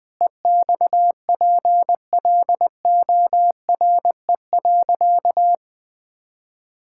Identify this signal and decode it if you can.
Morse code, heard as a keyed tone, about 20 wpm, 700 Hz. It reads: EXPLORE.
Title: Sentences with 6-Word Phrases - Spoken Beginning - 20wpm